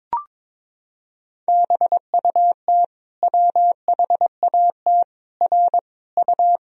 Morse code, heard as a keyed tone, about 22 wpm, 700 Hz. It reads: BUT WHAT R U
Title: Sentences from Top 100 Words and Abbreviations 22wpm